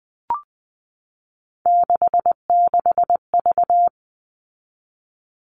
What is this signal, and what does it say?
Morse code, heard as a keyed tone, about 20 wpm, 700 Hz. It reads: 66V
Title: All Cut Numbers - 3-Digit 20wpm